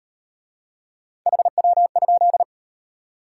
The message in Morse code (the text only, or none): HW?